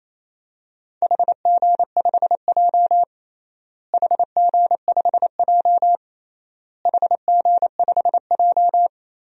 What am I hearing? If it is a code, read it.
Morse code, heard as a keyed tone, about 28 wpm, 700 Hz. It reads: HG5J HG5J HG5J
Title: DXCC-100 Call Signs - 4 Characters - Repeated 3x - 28wpm